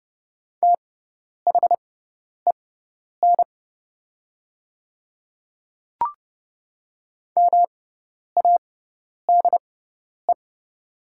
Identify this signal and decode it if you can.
Morse code, heard as a keyed tone, about 30 wpm, spaced as for 10 wpm, 700 Hz. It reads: THEN MADE
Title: 6x Character Spacing - Top 200 Words - 30wpm